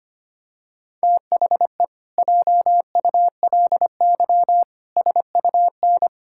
Morse code, heard as a keyed tone, about 25 wpm, 700 Hz. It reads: THE JULY SUN
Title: Pangram Challenge v2 - 25wpm